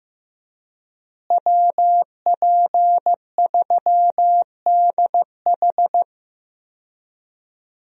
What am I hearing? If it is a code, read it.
Morse code, heard as a keyed tone, about 15 wpm, 700 Hz. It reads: WP3DH
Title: USA Call Signs - 5 Characters - Repeated 2x - 15wpm